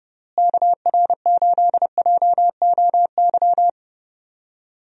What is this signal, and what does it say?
Morse code, heard as a keyed tone, about 30 wpm, 700 Hz. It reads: KR8JOY
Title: USA Call Signs - 6 Characters - Repeated 3x - 30wpm